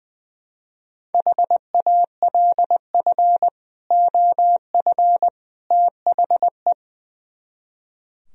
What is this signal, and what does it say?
Morse code, heard as a keyed tone, about 20 wpm, 700 Hz. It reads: HALF OF THE
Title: Speed-Racing - Sets of 3 Words 20wpm